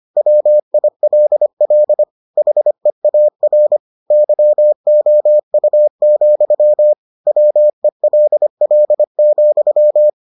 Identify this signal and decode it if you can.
Morse code, heard as a keyed tone, about 25 wpm, 600 Hz. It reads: WILL HEAR YOU, WELL,